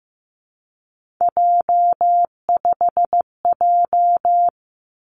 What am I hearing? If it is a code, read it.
Morse code, heard as a keyed tone, about 15 wpm, 700 Hz. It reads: J5J
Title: Three Challenging Letter-Number Combinations 15wpm